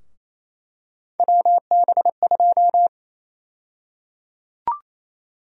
Morse code, heard as a keyed tone, about 28 wpm, 700 Hz. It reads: WB2
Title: USA Call Sign Prefixes - Repeated 2x - 28wpm